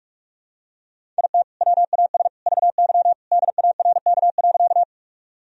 Morse code, heard as a keyed tone, about 46 wpm, 700 Hz. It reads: IT WAS VY DARK.
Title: Sentences from Top 500 Words and Abbreviations 45wpm